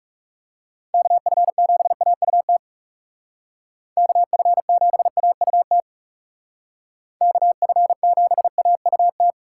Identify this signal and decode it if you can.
Morse code, heard as a keyed tone, about 39 wpm, 700 Hz. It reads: KF7AUT KF7AUT KF7AUT